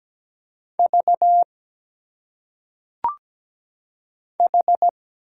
Morse code, heard as a keyed tone, about 17 wpm, 700 Hz. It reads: V H
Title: Single Letters - Phonetic Pronunciation 17wpm